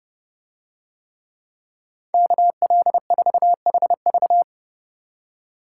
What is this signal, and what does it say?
Morse code, heard as a keyed tone, about 30 wpm, 700 Hz. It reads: KL4HV